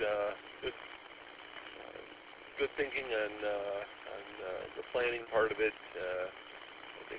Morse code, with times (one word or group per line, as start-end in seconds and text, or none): none